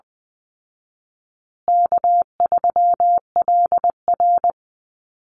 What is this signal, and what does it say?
Morse code, heard as a keyed tone, about 20 wpm, 700 Hz. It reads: K3LR
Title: Call Signs Repeated 3x - 20wpm